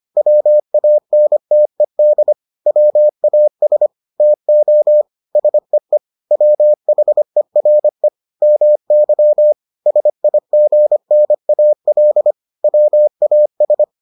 Morse code, heard as a keyed tone, about 25 wpm, 600 Hz. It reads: WANTED WAS TO SEE WHERE MY SIGNAL WAS